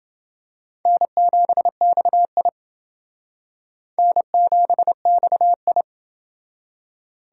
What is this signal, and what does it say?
Morse code, heard as a keyed tone, about 29 wpm, 700 Hz. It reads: N7XI N7XI